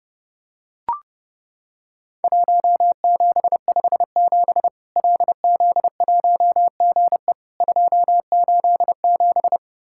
Morse code, heard as a keyed tone, about 32 wpm, 700 Hz. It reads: 1757 LZ1GE 287
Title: Worked All Europe DX Contest - QTC Exchange - 30wpm